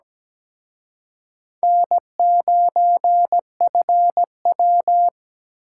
Morse code, heard as a keyed tone, about 20 wpm, 700 Hz. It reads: N9FW
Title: Speed Racing - USA Call Signs 17wpm